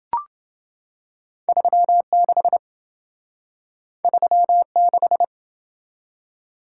Morse code, heard as a keyed tone, about 27 wpm, 700 Hz. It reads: 36 36